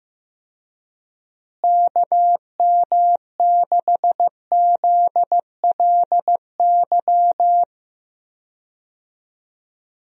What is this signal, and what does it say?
Morse code, heard as a keyed tone, about 15 wpm, 700 Hz. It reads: KM6ZLY
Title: USA Call Signs - 6 Characters - Repeated 2x - 15wpm